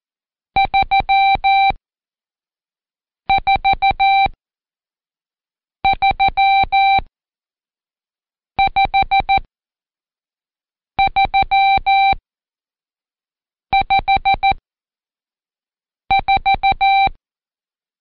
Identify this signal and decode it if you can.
Morse code, heard as a keyed tone, about 14 wpm, spaced as for 5 wpm, 780 Hz. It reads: 3435354
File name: Code Quick / AudioDownld1 / Seg79-Day9-Intro 5